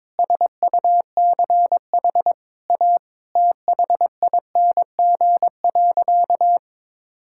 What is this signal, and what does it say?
Morse code, heard as a keyed tone, about 22 wpm, 700 Hz. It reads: SUCH A THING.
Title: Sentences from Top 1600 Words 22wpm